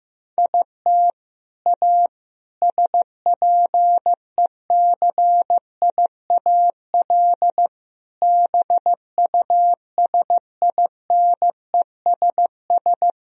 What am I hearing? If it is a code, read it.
Morse code, heard as a keyed tone, about 15 wpm, 700 Hz. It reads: IT A SPECIAL BUSINESS
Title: Sentences from Top 1000 Words - Review Part 3 - 15wpm